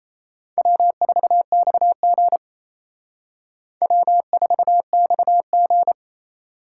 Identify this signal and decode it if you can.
Morse code, heard as a keyed tone, about 36 wpm, 700 Hz. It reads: W4XG W4XG